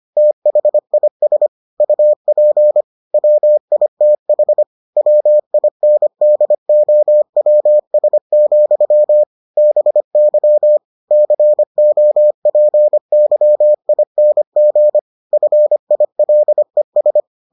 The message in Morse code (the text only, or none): THIS UP WITH WINDOWS, BY COPYING FILES